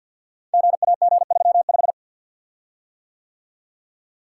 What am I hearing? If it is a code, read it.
Morse code, heard as a keyed tone, about 53 wpm, 700 Hz. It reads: GAG35
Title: Code Groups - 5 Letter:Numbers - 50wpm